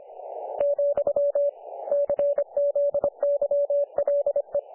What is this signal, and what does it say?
Morse code, heard as a keyed tone, about 25 wpm, 500 Hz. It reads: , CZYLE
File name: dlazaawansowanych / nagraniazpasma / websdr recording 3564.7kHz